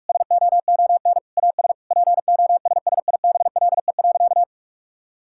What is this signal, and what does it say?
Morse code, heard as a keyed tone, about 46 wpm, 700 Hz. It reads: SOON AS POSSIBLE.
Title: Sentences from Top 1000 Words - Review Part 5 - 45wpm